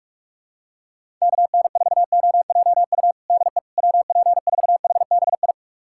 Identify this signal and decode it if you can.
Morse code, heard as a keyed tone, about 46 wpm, 700 Hz. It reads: KN4OJU DE WP4HBI
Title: USA Callsign DE Callsign 45wpm